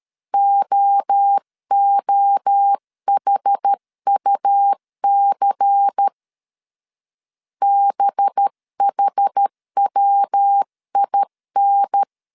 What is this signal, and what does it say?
Morse code, heard as a keyed tone, about 13 wpm, 800 Hz. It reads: OOHUC BHWIN
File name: CW / DET / 1 6 1500C S C 12wpm 3533